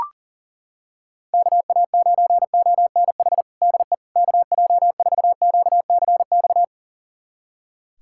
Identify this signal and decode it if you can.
Morse code, heard as a keyed tone, about 40 wpm, 700 Hz. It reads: KA9ONH DE KJ4QCX
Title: USA Callsign DE Callsign - 6 Characters - 40wpm